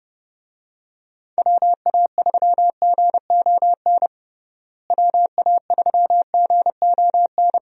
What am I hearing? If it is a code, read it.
Morse code, heard as a keyed tone, about 32 wpm, 700 Hz. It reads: WA3GON WA3GON
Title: USA Call Signs - 6 Characters - Repeated 2x - 30wpm